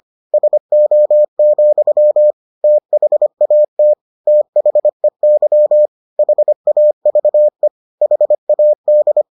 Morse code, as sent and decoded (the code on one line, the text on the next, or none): ... --- --..-- / - .... .- - / - .... -.-- / .... .- ...- . / .... .- -..
SO, THAT THEY HAVE HAD